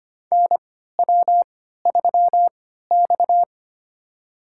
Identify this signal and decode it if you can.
Morse code, heard as a keyed tone, about 25 wpm, spaced as for 14 wpm, 700 Hz. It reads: NW3X
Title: USA Call Signs - 4 Characters - 3x Character Spacing - 25wpm